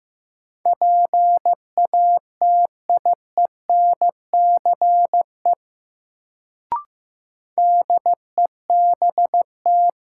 Morse code, heard as a keyed tone, about 15 wpm, 700 Hz. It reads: PATIENCE DEBT
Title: Sentences from Top 1300 Words - New Word Drill Alt - 15wpm